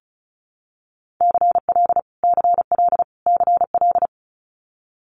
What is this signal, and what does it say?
Morse code, heard as a keyed tone, about 36 wpm, 700 Hz. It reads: CL CL CL
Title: Multi-letter Phonemes Repeated 3x 35wpm